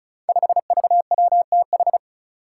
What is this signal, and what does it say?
Morse code, heard as a keyed tone, about 36 wpm, 700 Hz. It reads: 5VWTH
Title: Five Challenging Letter-Number Combinations 35wpm